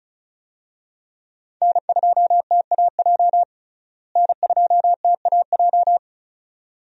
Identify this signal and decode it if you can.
Morse code, heard as a keyed tone, about 36 wpm, 700 Hz. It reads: N2TAJ N2TAJ
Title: USA Call Signs - 5 Characters - Repeated 2x - 35wpm